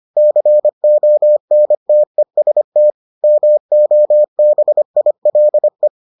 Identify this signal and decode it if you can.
Morse code, heard as a keyed tone, about 25 wpm, 600 Hz. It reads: CONTEST MOBILE